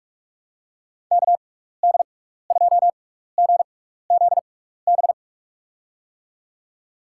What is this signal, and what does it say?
Morse code, heard as a keyed tone, about 44 wpm, spaced as for 15 wpm, 700 Hz. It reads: KD2GZB
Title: USA Call Signs - 6 Characters - 6x Character Spacing - 45wpm